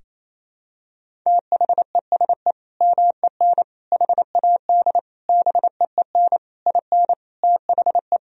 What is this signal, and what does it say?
Morse code, heard as a keyed tone, about 28 wpm, 700 Hz. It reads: THESE MEN HAD BEEN IN THE
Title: Sentences from Top 1000 Words - Review Part 1 - 28wpm